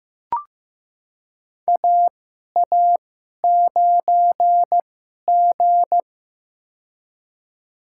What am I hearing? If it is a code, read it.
Morse code, heard as a keyed tone, about 15 wpm, spaced as for 11 wpm, 700 Hz. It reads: AA9G
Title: USA Call Signs - 2x Character Spacing - 15wpm